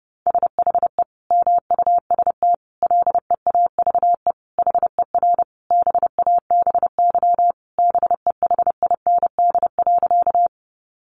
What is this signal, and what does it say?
Morse code, heard as a keyed tone, about 30 wpm, 700 Hz. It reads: SHE MUST LEAVE HER BABY BEHIND.
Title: Sentences from Top 1300 Words 30wpm